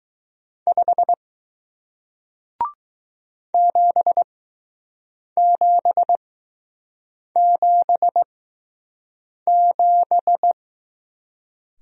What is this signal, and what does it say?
Morse code, heard as a keyed tone, about 20 wpm, spaced as for 6 wpm, 700 Hz. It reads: 5 7777